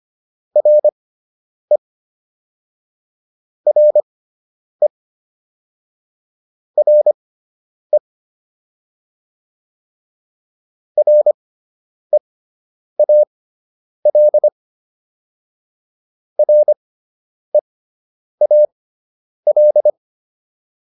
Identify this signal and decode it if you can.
Morse code, heard as a keyed tone, about 25 wpm, spaced as for 9 wpm, 600 Hz. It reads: RE RE RE REAL REAL